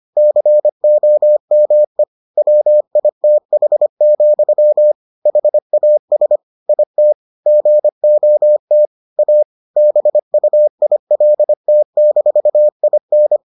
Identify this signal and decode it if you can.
Morse code, heard as a keyed tone, about 25 wpm, 600 Hz. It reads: COME WITH, HAS IT GOT A BUILT-IN